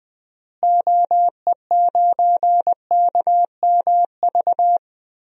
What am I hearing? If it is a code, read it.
Morse code, heard as a keyed tone, about 20 wpm, 700 Hz. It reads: OE9KMV
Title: DXCC-100 Call Signs - 6 Characters - Repeated 3x - 20wpm